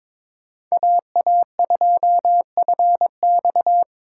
Morse code, heard as a keyed tone, about 21 wpm, 700 Hz. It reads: AA2FX
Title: USA Call Signs - 5 Characters - 2x Character Spacing - 22wpm